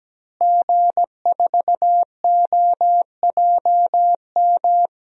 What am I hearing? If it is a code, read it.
Morse code, heard as a keyed tone, about 17 wpm, 700 Hz. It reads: G4OJM
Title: DXCC-100 Call Signs - 5 Characters - Repeated 3x - 17wpm